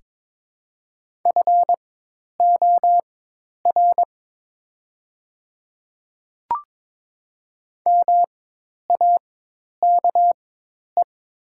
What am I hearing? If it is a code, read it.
Morse code, heard as a keyed tone, about 22 wpm, spaced as for 10 wpm, 700 Hz. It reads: FOR MAKE
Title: 4x Character Spacing - Top 100 Words - 22wpm